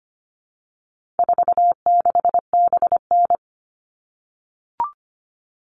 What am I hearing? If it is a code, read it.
Morse code, heard as a keyed tone, about 25 wpm, 700 Hz. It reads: 46BN
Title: All Cut Numbers - 4-Digit 25wpm